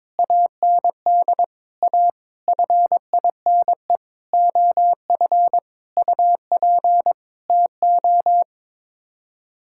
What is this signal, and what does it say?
Morse code, heard as a keyed tone, about 22 wpm, 700 Hz. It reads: AND A FINE OF UP TO
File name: Sets of 9 Words - 22wpm